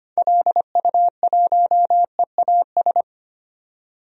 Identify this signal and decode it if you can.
Morse code, heard as a keyed tone, about 25 wpm, 700 Hz. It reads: LU1EAS
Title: DXCC-100 Call Signs - 6 Characters - 25wpm